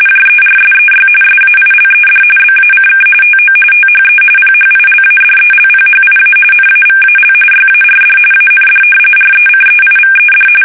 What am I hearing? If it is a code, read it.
RTTY, about 45.45 baud. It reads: JERSEY SAILED BACK TO THE PHILADELPHIA NAVAL YARD AND RECEIVED
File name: fsk850 / battleshipnj